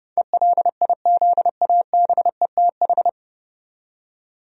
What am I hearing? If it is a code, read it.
Morse code, heard as a keyed tone, about 30 wpm, 700 Hz. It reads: ELIZABETH
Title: USA Common Names - 30wpm